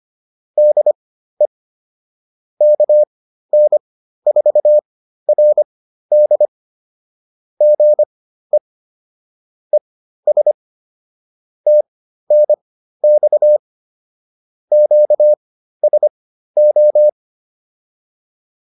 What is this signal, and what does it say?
Morse code, heard as a keyed tone, about 25 wpm, spaced as for 13 wpm, 600 Hz. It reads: DE KN4RD GE ES TNX QSO
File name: QSO 207 13